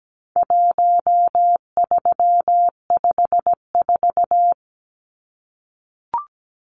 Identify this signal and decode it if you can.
Morse code, heard as a keyed tone, about 17 wpm, 700 Hz. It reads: 1354
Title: Numbers - 4-Digit 17wpm